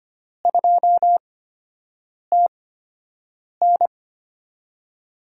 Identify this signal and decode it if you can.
Morse code, heard as a keyed tone, about 25 wpm, spaced as for 7 wpm, 700 Hz. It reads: 2TN
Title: USA Call Signs - 8x Character Spacing - 25wpm